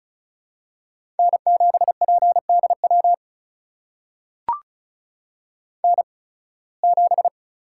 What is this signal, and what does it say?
Morse code, heard as a keyed tone, about 36 wpm, spaced as for 23 wpm, 700 Hz. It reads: N7PDW N7
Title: USA Call Signs - 5 Characters - 8x Character Spacing - 35wpm